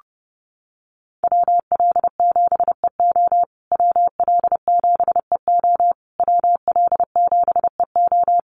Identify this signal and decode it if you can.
Morse code, heard as a keyed tone, about 30 wpm, 700 Hz. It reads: WL7EO WL7EO WL7EO
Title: USA Call Signs Repeated 3x 30wpm